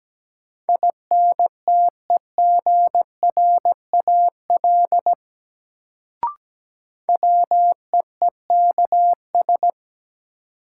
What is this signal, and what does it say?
Morse code, heard as a keyed tone, about 17 wpm, 700 Hz. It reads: INTEGRAL WEEKS
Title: Sets of 5 Words - New Word Drill - 17wpm